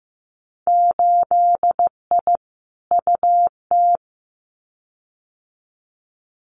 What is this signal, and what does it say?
Morse code, heard as a keyed tone, about 15 wpm, 700 Hz. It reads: 8I UT
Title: Winter Field Day Exchanges Encore 15wpm